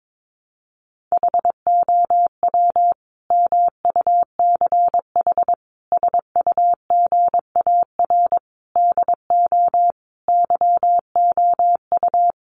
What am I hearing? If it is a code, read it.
Morse code, heard as a keyed tone, about 22 wpm, 700 Hz. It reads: HOW MUCH SUGAR DO YOU
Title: Sentences from Top 1200 Words 22wpm